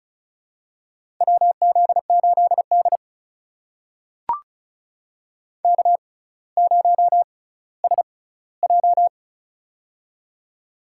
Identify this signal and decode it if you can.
Morse code, heard as a keyed tone, about 36 wpm, spaced as for 18 wpm, 700 Hz. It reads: WZ8D K0SJ
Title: USA Call Signs - 4 Characters - 6x Character Spacing - 35wpm